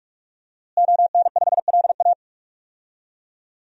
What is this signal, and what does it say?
Morse code, heard as a keyed tone, about 46 wpm, 700 Hz. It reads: ON5LA
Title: DXCC-100 Call Signs - 5 Characters - Repeated 3x - 45wpm